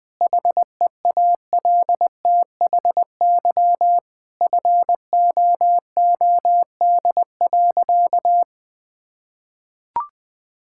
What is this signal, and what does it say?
Morse code, heard as a keyed tone, about 20 wpm, 700 Hz. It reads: HEALTHY FOOD.
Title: Sentences from Top 1000 Words - Review Part 5 - 20wpm